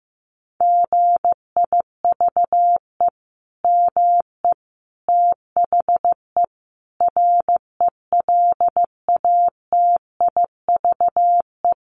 Morse code, heard as a keyed tone, about 15 wpm, 700 Hz. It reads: GIVE ME THE RELATIVE
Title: Sentences from Top 1400 Words 15wpm